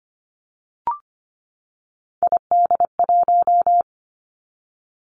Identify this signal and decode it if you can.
Morse code, heard as a keyed tone, about 25 wpm, 700 Hz. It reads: ID1
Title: Three Challenging Letter-Number Combinations 25wpm